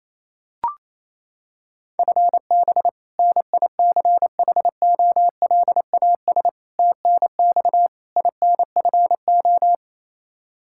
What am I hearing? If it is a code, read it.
Morse code, heard as a keyed tone, about 28 wpm, 700 Hz. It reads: FB NICHOLAS TNX INFO
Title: K1USN SST QSO Practice 28wpm